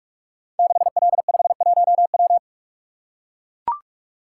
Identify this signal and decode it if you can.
Morse code, heard as a keyed tone, about 44 wpm, 700 Hz. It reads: 6L51W
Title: Code Groups - 5 Letter:Numbers - 45wpm